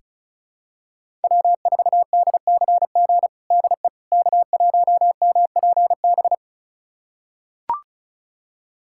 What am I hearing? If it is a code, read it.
Morse code, heard as a keyed tone, about 34 wpm, 700 Hz. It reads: W4DCG DE K1MPB
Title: USA Callsign DE Callsign - 5 Characters - 35wpm